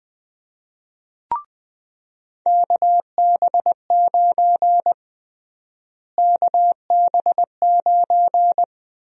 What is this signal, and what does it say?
Morse code, heard as a keyed tone, about 21 wpm, 700 Hz. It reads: KB9 KB9
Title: USA Call Sign Prefixes - Repeated 2x - 20wpm